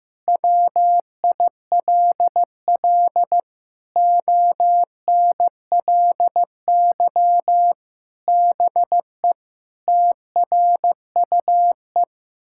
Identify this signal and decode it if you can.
Morse code, heard as a keyed tone, about 15 wpm, 700 Hz. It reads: WILL ONLY BE TRUE
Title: Sentences from Top 1000 Words - Review Part 6 - 15wpm